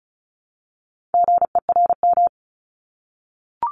Morse code, heard as a keyed tone, about 36 wpm, 700 Hz. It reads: GERM
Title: Multi-letter Phonemes Repeated 3x 35wpm